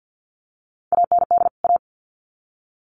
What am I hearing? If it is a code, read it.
Morse code, heard as a keyed tone, about 50 wpm, 700 Hz. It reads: AND A
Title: Sets of 2 Words 50wpm